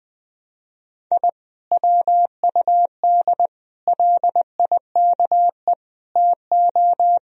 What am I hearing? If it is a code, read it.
Morse code, heard as a keyed tone, about 20 wpm, 700 Hz. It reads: I WUD LIKE TO
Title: Sentences from Top 200 Words and Abbreviations 20wpm